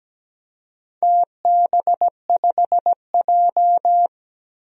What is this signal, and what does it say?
Morse code, heard as a keyed tone, about 17 wpm, 700 Hz. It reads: TB5J
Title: DXCC-100 Call Signs - 4 Characters - Repeated 2x - 17wpm